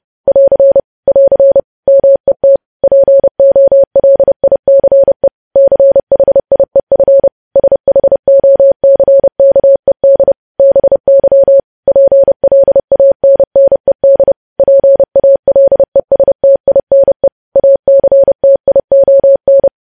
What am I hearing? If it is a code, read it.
Morse code, heard as a keyed tone, about 30 wpm, 550 Hz. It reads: <AR> <AR> MET POLICE CHIEF SHOCKED BY PLANNED PALESTINE ACTION
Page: news headlines in morse code at 30 wpm